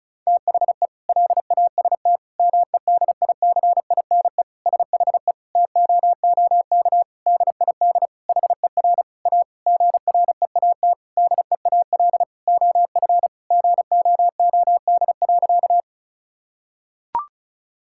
Morse code, heard as a keyed tone, about 36 wpm, 700 Hz. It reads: THE LAST MEDICINE SHE TOOK DID HER A GREAT DEAL OF GOOD.
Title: Sentences from Top 1000 Words - Review Part 6 - 35wpm